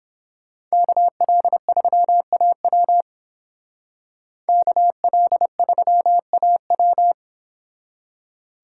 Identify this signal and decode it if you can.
Morse code, heard as a keyed tone, about 30 wpm, 700 Hz. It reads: KL3AW KL3AW